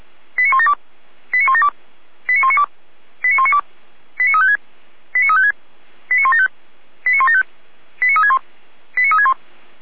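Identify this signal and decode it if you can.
Five-tone selective call (ZVEI), heard as quick runs of five tones, tones of about 70 ms. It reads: 89172 89172 89192 89192 89356 89356 89175 89175 89361 89361
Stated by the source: ZVEI five-tone call 89018 received on an SDR.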